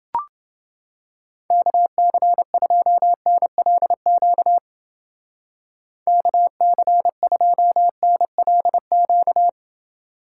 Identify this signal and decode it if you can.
Morse code, heard as a keyed tone, about 28 wpm, 700 Hz. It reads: KC2NLQ KC2NLQ